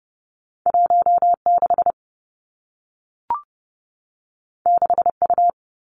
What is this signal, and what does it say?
Morse code, heard as a keyed tone, about 30 wpm, 700 Hz. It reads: 1 6 6 U
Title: Two Challenging Letter-Number Combinations 30wpm